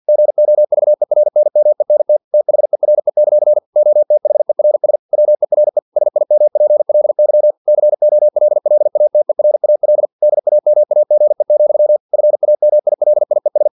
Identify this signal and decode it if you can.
Morse code, heard as a keyed tone, about 50 wpm, 600 Hz. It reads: GOVERNMENT THERE; OTHERS WERE SIMPLY COLLATERAL DAMAGE, FAMILIES
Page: Never 49wpm